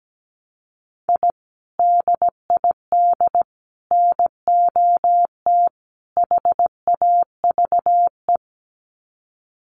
Morse code, heard as a keyed tone, about 17 wpm, 700 Hz. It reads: I DID NOT HAVE
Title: Sets of 4 Words 17wpm